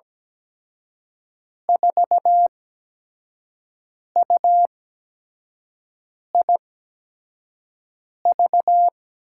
Morse code, heard as a keyed tone, about 17 wpm, 700 Hz. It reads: 4 U I V
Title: USA Call Signs - 5 Characters - 8x Character Spacing - 17wpm